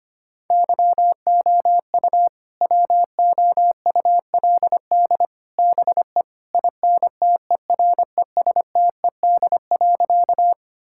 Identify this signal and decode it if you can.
Morse code, heard as a keyed tone, about 25 wpm, 700 Hz. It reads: YOU WOULD BE INTERESTED.
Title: Sentences from Top 1000 Words - Review Part 6 - 25wpm